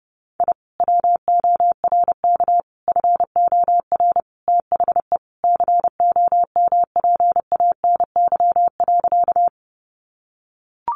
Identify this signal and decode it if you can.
Morse code, heard as a keyed tone, about 30 wpm, 700 Hz. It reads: I WORK FOR THE COMPANY.
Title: Sentences from Top 600 Words 30wpm